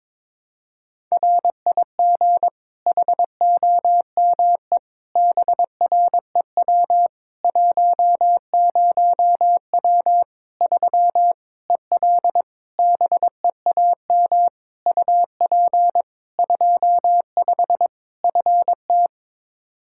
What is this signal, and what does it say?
Morse code, heard as a keyed tone, about 22 wpm, 700 Hz. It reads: RIG HOME BREW 10W 3 EL BEAM UP 25 FT
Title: K1USN SST QSO Practice 22wpm